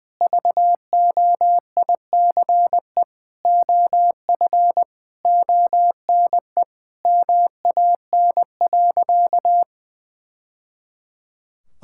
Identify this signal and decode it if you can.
Morse code, heard as a keyed tone, about 20 wpm, 700 Hz. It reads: VOICE OF ONE MAN.
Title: Sentences from Top 1000 Words - Review Part 6 - 20wpm